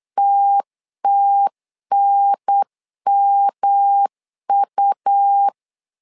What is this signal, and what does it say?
Morse code, heard as a keyed tone, about 8 wpm, 800 Hz. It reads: TTNMU
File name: LEZIONI / 3 MWRO / 3 M 8wpm 2300